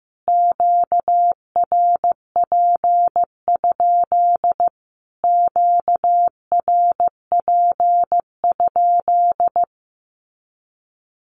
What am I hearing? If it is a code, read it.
Morse code, heard as a keyed tone, about 15 wpm, 700 Hz. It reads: QRP? QRP?
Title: Q-Codes Repeated 3x 15wpm